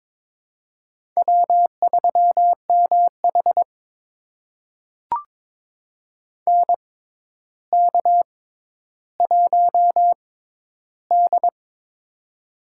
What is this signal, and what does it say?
Morse code, heard as a keyed tone, about 22 wpm, spaced as for 11 wpm, 700 Hz. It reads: W3MH NK1D
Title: USA Call Signs - 4 Characters - 6x Character Spacing - 22wpm